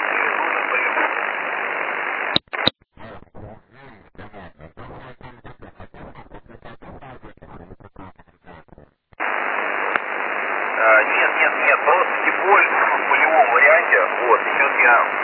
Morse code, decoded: none

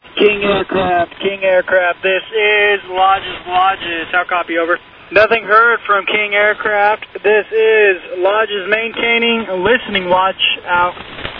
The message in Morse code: none

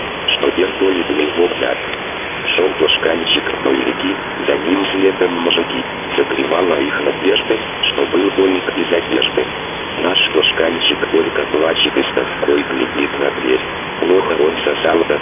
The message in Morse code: none

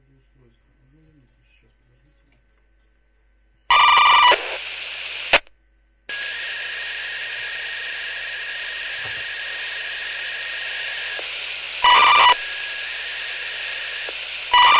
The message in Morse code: none

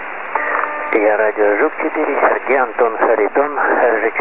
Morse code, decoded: none

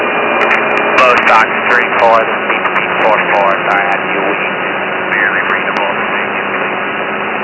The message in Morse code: none

